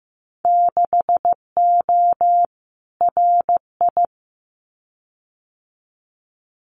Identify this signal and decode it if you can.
Morse code, heard as a keyed tone, about 15 wpm, 700 Hz. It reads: 6O RI
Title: Winter Field Day Exchanges Encore 15wpm